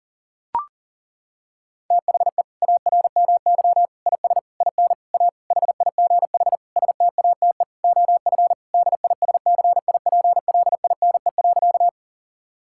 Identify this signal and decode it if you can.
Morse code, heard as a keyed tone, about 40 wpm, 700 Hz. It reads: THE ARMY IS IN A HIGH STATE OF DISCIPLINE.
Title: Sentences from Top 1000 Words - Review Part 6 - 40wpm